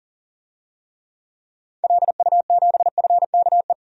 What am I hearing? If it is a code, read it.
Morse code, heard as a keyed tone, about 40 wpm, 700 Hz. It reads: LU7FKE
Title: DXCC-100 Call Signs - 6 Characters - 40wpm